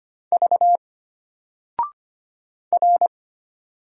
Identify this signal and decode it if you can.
Morse code, heard as a keyed tone, about 25 wpm, 700 Hz. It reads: V R
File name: Single-Letter-Number-25wpm